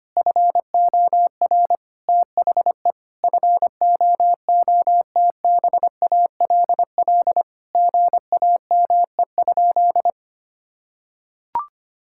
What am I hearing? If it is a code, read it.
Morse code, heard as a keyed tone, about 25 wpm, 700 Hz. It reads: FOR THE FOOTBALL GAME?
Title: Sentences from Top 1600 Words 25wpm